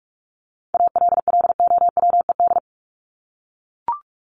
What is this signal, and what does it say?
Morse code, heard as a keyed tone, about 44 wpm, 700 Hz. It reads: ALLOWED
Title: ED Suffix - Top 100 Words with Base Words 45wpm